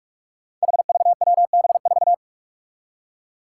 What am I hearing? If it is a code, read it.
Morse code, heard as a keyed tone, about 44 wpm, 700 Hz. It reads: HVWB4
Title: Five Challenging Letter-Number Combinations 45wpm